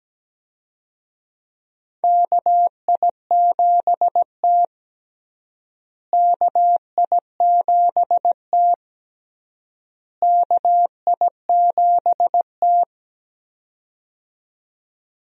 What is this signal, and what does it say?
Morse code, heard as a keyed tone, about 17 wpm, 700 Hz. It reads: KI7T KI7T KI7T
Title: USA Call Signs - 4 Characters - Repeated 3x - 17wpm